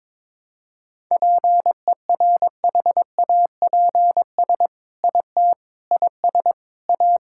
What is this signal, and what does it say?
Morse code, heard as a keyed tone, about 22 wpm, 700 Hz. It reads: PERHAPS IT IS A
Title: Sentences from Top 1000 Words - Review Part 5 - 22wpm